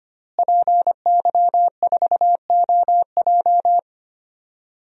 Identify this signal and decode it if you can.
Morse code, heard as a keyed tone, about 25 wpm, 700 Hz. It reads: PY4OJ
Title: DXCC-100 Call Signs - 5 Characters - Repeated 3x - 25wpm